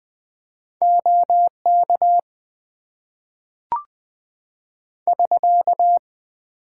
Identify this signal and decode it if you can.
Morse code, heard as a keyed tone, about 20 wpm, 700 Hz. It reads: OK <SK>
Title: Top 90 QSO Elements - 8x Character Spacing - 20wpm